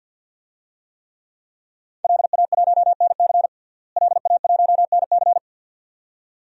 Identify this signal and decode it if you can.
Morse code, heard as a keyed tone, about 52 wpm, 700 Hz. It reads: LA1NC LA1NC
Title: DXCC-100 Call Signs - 5 Characters - Repeated 2x - 50wpm